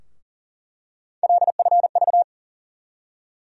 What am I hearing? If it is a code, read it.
Morse code, heard as a keyed tone, about 40 wpm, 700 Hz. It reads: LFV
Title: USA Call Sign Postfixes - 3 Letters 40wpm